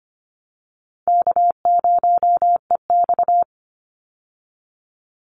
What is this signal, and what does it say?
Morse code, heard as a keyed tone, about 25 wpm, 700 Hz. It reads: K0EX
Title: Easy Call Signs Repeated 2x - 25wpm